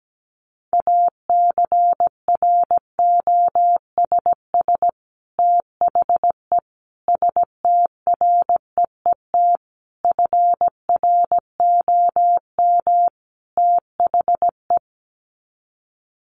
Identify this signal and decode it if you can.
Morse code, heard as a keyed tone, about 17 wpm, 700 Hz. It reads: ACROSS THE STREET FROM THE
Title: Sets of 5 Words Encore 17wpm